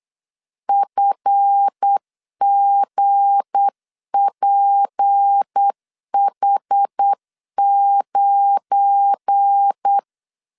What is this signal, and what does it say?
Morse code, heard as a keyed tone, about 8 wpm, 800 Hz. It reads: FGPH9
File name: LEZIONI / 9 28 8wpm 3512